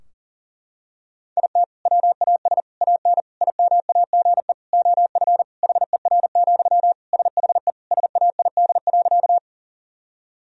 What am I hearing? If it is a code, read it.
Morse code, heard as a keyed tone, about 37 wpm, 700 Hz. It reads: IT WAS AN IMAGE OF HER, SHE SAID.
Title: Sentences from Top 1000 Words - Review Part 4 - 40wpm